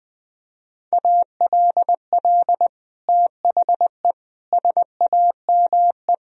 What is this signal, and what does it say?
Morse code, heard as a keyed tone, about 20 wpm, 700 Hz. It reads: ALL THE SAME
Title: Sentences from Top 1000 Words - Review Part 2 - 20wpm